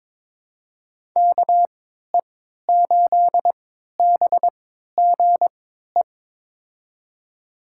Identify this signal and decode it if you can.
Morse code, heard as a keyed tone, about 22 wpm, spaced as for 12 wpm, 700 Hz. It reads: KE8BGE
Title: USA Call Signs - 6 Characters - 3x Character Spacing - 22wpm